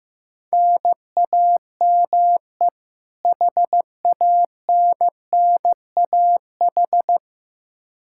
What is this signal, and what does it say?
Morse code, heard as a keyed tone, about 15 wpm, 700 Hz. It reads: NAME HANNAH
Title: Practice QSO - Exchange 1 - Beginning - 15wpm